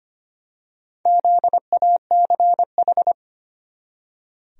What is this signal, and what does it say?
Morse code, heard as a keyed tone, about 25 wpm, 700 Hz. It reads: ZACH
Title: USA Common Names - Max 4 Letters - 25wpm